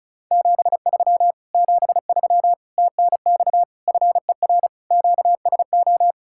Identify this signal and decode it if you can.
Morse code, heard as a keyed tone, about 46 wpm, 700 Hz. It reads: 73 73 TNX FER QSO